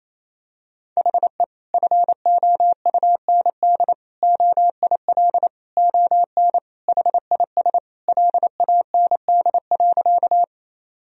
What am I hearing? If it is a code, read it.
Morse code, heard as a keyed tone, about 27 wpm, 700 Hz. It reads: HE FOUND OIL ON HIS LAND.
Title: Sentences from Top 1000 Words - Review Part 4 - 28wpm